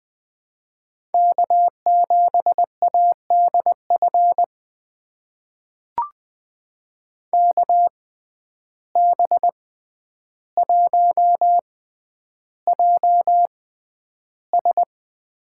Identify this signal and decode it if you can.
Morse code, heard as a keyed tone, about 21 wpm, spaced as for 10 wpm, 700 Hz. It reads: K7ADF KB1JS
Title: USA Call Signs - 5 Characters - 6x Character Spacing - 20wpm